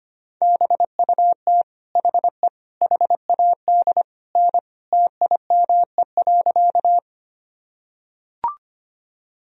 Morse code, heard as a keyed tone, about 25 wpm, 700 Hz. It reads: BUT HE HAD N TIME.
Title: Sentences from Top 100 Words and Abbreviations 25wpm